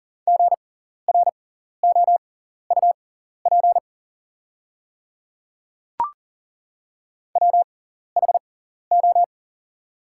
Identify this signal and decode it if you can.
Morse code, heard as a keyed tone, about 40 wpm, spaced as for 14 wpm, 700 Hz. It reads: GROUP WHO
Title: 6x Character Spacing - Top 500 Words - 40wpm